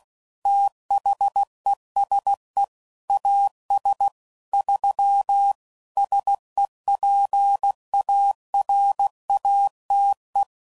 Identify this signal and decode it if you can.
Morse code, heard as a keyed tone, about 16 wpm, 800 Hz. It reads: THESE AS 3 SEPARATE
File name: CW Code Groups @15WPM / Technical Subjects / PwrSupply